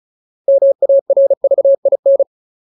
Morse code, heard as a keyed tone, about 34 wpm, 550 Hz. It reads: MARVIN
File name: Ultimate Word List / 35WPM Ultimate